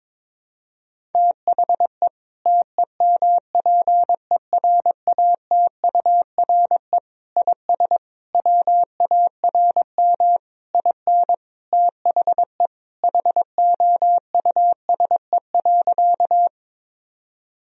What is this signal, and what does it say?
Morse code, heard as a keyed tone, about 22 wpm, 700 Hz. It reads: THE TEMPERATURE IS WARM IN THE HOUSE.
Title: Sentences from Top 1000 Words - Review Part 4 - 22wpm